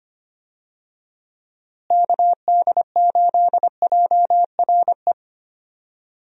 Morse code, heard as a keyed tone, about 25 wpm, 700 Hz. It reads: KD8JRE
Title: USA Call Signs - 6 Characters - Repeated 3x - 25wpm